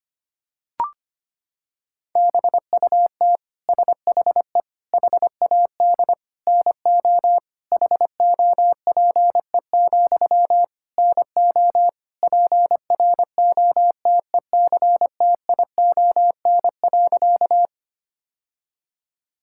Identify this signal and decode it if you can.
Morse code, heard as a keyed tone, about 25 wpm, 700 Hz. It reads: BUT SHE HAD NO HOPE, NO PROTECTION.
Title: Sentences from Top 1600 Words 25wpm